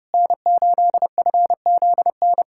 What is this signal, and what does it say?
Morse code, heard as a keyed tone, about 32 wpm, 700 Hz. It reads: N8FZN
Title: USA Callsign DE Callsign - 5 Characters - 30wpm